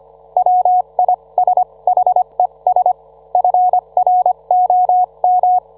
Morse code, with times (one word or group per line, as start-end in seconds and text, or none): 0.4-2.9 WISHES
3.3-5.6 FROM